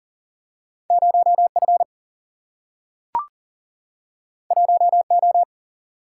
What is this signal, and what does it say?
Morse code, heard as a keyed tone, about 40 wpm, 700 Hz. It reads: 0 F 1 O
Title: Single-Letter Single-Number 40wpm